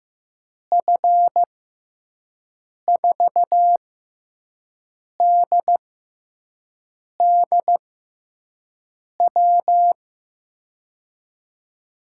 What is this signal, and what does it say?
Morse code, heard as a keyed tone, about 15 wpm, spaced as for 5 wpm, 700 Hz. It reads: F4DDW
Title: USA Call Signs - 6 Characters - 6x Character Spacing - 15wpm